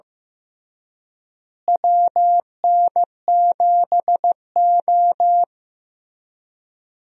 Morse code, heard as a keyed tone, about 15 wpm, 700 Hz. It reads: WN7O
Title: USA Call Signs - 4 Characters - Repeated 3x - 15wpm